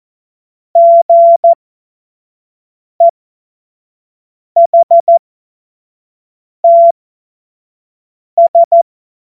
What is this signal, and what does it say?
Morse code, heard as a keyed tone, about 14 wpm, spaced as for 5 wpm, 680 Hz. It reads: GEHTS
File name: Morsekurs / Witze / Witz 09 April 2024 25 F70